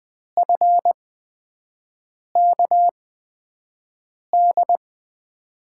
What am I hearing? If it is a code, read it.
Morse code, heard as a keyed tone, about 20 wpm, spaced as for 5 wpm, 700 Hz. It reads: FKD
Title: USA Call Signs - 6 Characters - 8x Character Spacing - 20wpm